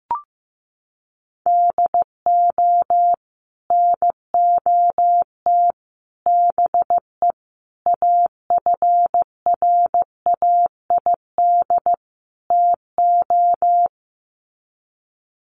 Morse code, heard as a keyed tone, about 14 wpm, 700 Hz. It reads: DO NOT BE AFRAID TO
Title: Sets of 5 Words Encore2 15wpm